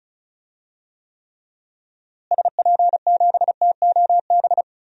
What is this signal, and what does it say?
Morse code, heard as a keyed tone, about 36 wpm, 700 Hz. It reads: SP7TOB
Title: DXCC-100 Call Signs - 6 Characters - Repeated 3x - 35wpm